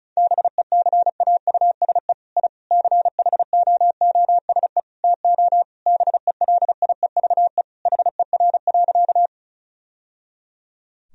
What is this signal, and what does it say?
Morse code, heard as a keyed tone, about 36 wpm, 700 Hz. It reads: BECAUSE I CHOOSE TO BELIEVE HER.
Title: Sentences from Top 1000 Words - Review Part 4 - 35wpm